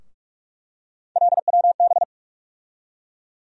Morse code, heard as a keyed tone, about 44 wpm, 700 Hz. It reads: LWB
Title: USA Call Sign Postfixes - 3 Letters 45wpm